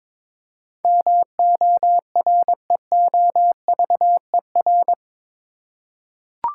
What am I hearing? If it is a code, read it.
Morse code, heard as a keyed tone, about 22 wpm, 700 Hz. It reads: MOREOVER
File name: practice / 22wpm / Eight Letter Words - 22wpm